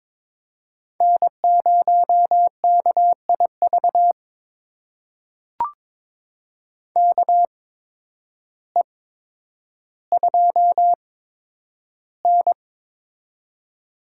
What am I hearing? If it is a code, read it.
Morse code, heard as a keyed tone, about 22 wpm, spaced as for 10 wpm, 700 Hz. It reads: N0KIV KE2N
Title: USA Call Signs - 8x Character Spacing - 22wpm